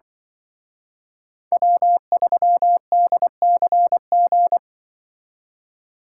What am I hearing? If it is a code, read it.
Morse code, heard as a keyed tone, about 24 wpm, 700 Hz. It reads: W3DCG
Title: Speed Racing - USA 1x3 Call Signs 20wpm